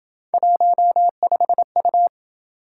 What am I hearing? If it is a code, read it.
Morse code, heard as a keyed tone, about 27 wpm, 700 Hz. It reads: 15U